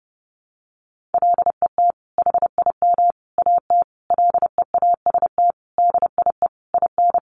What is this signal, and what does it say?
Morse code, heard as a keyed tone, about 28 wpm, 700 Hz. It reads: LET HIM AT LEAST DIE IN
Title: Sentences from Top 1500 Words 30wpm